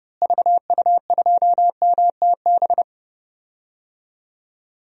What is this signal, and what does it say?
Morse code, heard as a keyed tone, about 30 wpm, 700 Hz. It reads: VU2MTB
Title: DXCC-100 Call Signs - 6 Characters - Repeated 3x - 30wpm